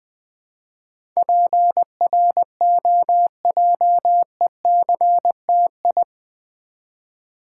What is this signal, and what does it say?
Morse code, heard as a keyed tone, about 20 wpm, 700 Hz. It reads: PROJECTI
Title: Build A Word - 4 Letters At A Time - 10-Letter Words - 20wpm